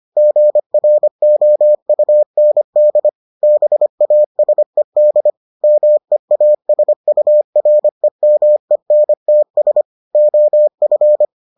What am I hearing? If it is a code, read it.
Morse code, heard as a keyed tone, about 25 wpm, 600 Hz. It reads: GROUND BASED MEASUREMENTS OF